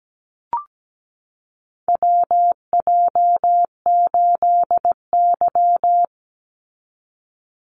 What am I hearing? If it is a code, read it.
Morse code, heard as a keyed tone, about 17 wpm, 700 Hz. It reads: WJ8Y
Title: Call Signs Repeated 2x - 17wpm